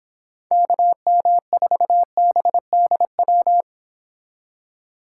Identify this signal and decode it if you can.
Morse code, heard as a keyed tone, about 30 wpm, 700 Hz. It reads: KM4BDW